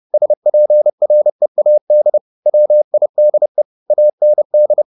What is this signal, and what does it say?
Morse code, heard as a keyed tone, about 30 wpm, 600 Hz. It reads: SPREAD WIDE AND
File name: the old man and the sea / the old man and the sea 1 30wpm